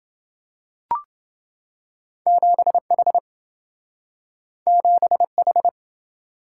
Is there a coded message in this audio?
Yes: Morse code from 2.3 to 5.7 s, heard as a keyed tone, about 27 wpm, 700 Hz. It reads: 7H 7H